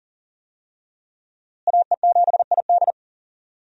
Morse code, heard as a keyed tone, about 40 wpm, 690 Hz. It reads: AE7ID
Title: USA Call Signs - 5 Characters - Repeated 3x - 40wpm